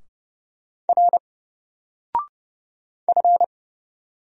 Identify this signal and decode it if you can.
Morse code, heard as a keyed tone, about 31 wpm, 700 Hz. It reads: R F
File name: Single-Letter-Number-30wpm